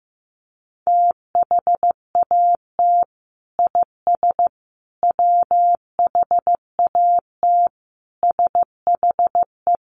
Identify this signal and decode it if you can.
Morse code, heard as a keyed tone, about 15 wpm, 700 Hz. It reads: THAT IS WHAT SHE
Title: Sentences from Top 100 Words 15wpm